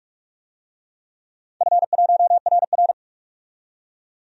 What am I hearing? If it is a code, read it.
Morse code, heard as a keyed tone, about 41 wpm, 700 Hz. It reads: F1RR
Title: DXCC-100 Call Signs - 4 Characters - 45wpm